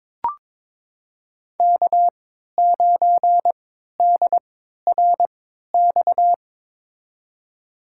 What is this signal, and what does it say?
Morse code, heard as a keyed tone, about 22 wpm, spaced as for 13 wpm, 700 Hz. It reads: K9DRX
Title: USA Call Signs - 3x Character Spacing - 22wpm